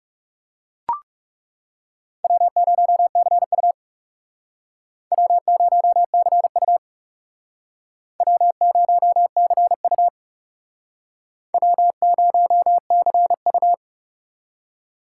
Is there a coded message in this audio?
Yes: Morse code from 2.2 to 13.7 s, heard as a keyed tone, about 39 wpm, 690 Hz. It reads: W0CU W0CU W0CU W0CU